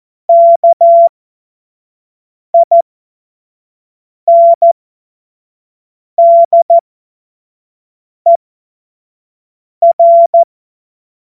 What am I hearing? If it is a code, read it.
Morse code, heard as a keyed tone, about 14 wpm, spaced as for 5 wpm, 680 Hz. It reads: KINDER
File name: Morsekurs / Witze / Witz 10 Juni 2025 25 F70